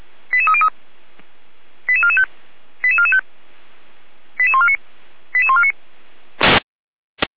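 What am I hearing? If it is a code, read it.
Five-tone selective call (ZVEI), heard as quick runs of five tones, tones of about 70 ms. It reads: 88303 88445 88445 88149 88149
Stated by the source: ZVEI five-tone selective call received with an SDR, logged on the Feuerwehr Orlamünde fire brigade server.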